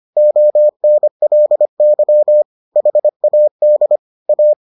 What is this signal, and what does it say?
Morse code, heard as a keyed tone, about 25 wpm, 600 Hz. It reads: ONLY HAD A